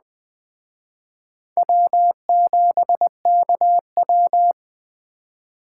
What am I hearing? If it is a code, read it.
Morse code, heard as a keyed tone, about 20 wpm, 700 Hz. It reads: W7KW
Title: Speed Racing - USA 1x2 Call Signs 17wpm